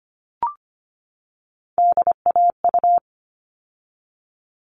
Morse code, heard as a keyed tone, about 25 wpm, 700 Hz. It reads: DAU
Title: All Cut Numbers - 3-Digit 25wpm